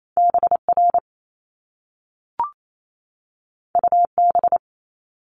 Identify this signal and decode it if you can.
Morse code, heard as a keyed tone, about 28 wpm, 700 Hz. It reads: BR UB
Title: Top Two Letter Combinations in Words 28wpm